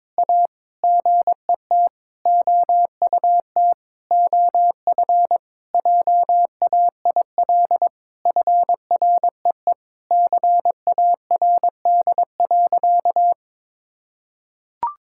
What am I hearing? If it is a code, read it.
Morse code, heard as a keyed tone, about 22 wpm, 700 Hz. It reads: A GET OUT OF JAIL FREE CARD.
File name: Sentences with 7-Word Phrases - 22wpm